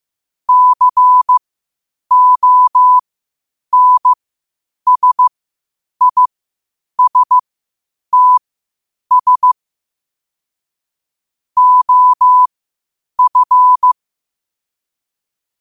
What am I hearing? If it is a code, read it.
Morse code, heard as a keyed tone, about 15 wpm, spaced as for 8 wpm, 1000 Hz. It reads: CONSISTS OF
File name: qotdfiles / 8wpm